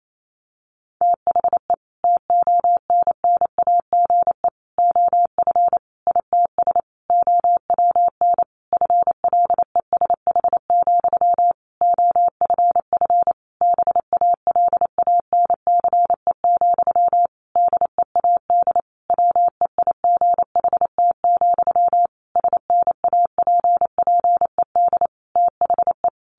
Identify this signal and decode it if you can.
Morse code, heard as a keyed tone, about 28 wpm, 700 Hz. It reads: THE TONNAGE OF ITS OWN FLESH, OFF BALANCE, DEAD WEIGHT, SNAPPED THE
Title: A Sound Of Thunder 28wpm